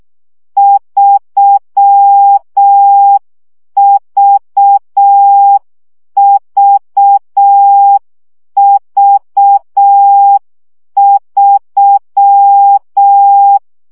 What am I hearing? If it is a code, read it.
Morse code, heard as a keyed tone, about 6 wpm, 800 Hz. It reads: 3VVV3
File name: CW gyakorlas v 3 5 baud